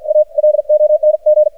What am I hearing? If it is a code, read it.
Morse code, heard as a keyed tone, about 54 wpm, spaced as for 47 wpm, 610 Hz. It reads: A LONG
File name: archived / CRSnet